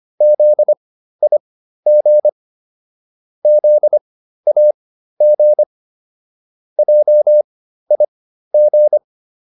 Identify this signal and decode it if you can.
Morse code, heard as a keyed tone, about 25 wpm, spaced as for 13 wpm, 600 Hz. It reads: ZIG ZAG JIG